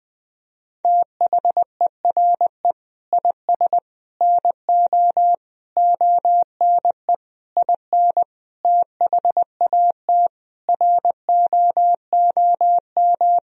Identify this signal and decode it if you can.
Morse code, heard as a keyed tone, about 20 wpm, 700 Hz. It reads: THERE IS NO ONE IN THAT ROOM
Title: Sentences from Top 500 Words and Abbreviations 20wpm